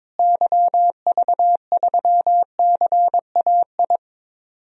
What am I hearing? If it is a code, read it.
Morse code, heard as a keyed tone, about 22 wpm, 700 Hz. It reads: YV3CAI
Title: DXCC-100 Call Signs - 6 Characters - Repeated 3x - 22wpm